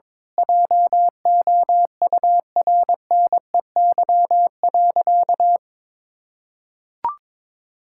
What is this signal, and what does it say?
Morse code, heard as a keyed tone, about 22 wpm, 700 Hz. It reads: JOURNEY.
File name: Sentences with 9-Word Phrases - 22wpm